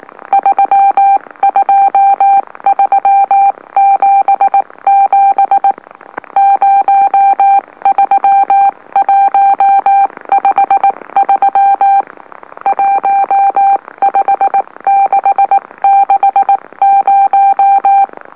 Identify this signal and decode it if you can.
Morse code, heard as a keyed tone, about 19 wpm, spaced as for 16 wpm, 790 Hz. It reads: 32377 03153 15660
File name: M04- -2345utc-LOLO